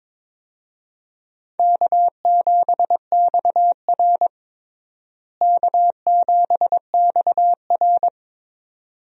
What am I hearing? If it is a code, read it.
Morse code, heard as a keyed tone, about 22 wpm, 700 Hz. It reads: K7XR K7XR
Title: USA Call Signs - 4 Characters - Repeated 2x - 22wpm